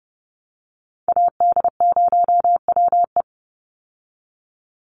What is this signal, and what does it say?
Morse code, heard as a keyed tone, about 30 wpm, 700 Hz. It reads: AD0WE
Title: Call Signs Repeated 2x - 30wpm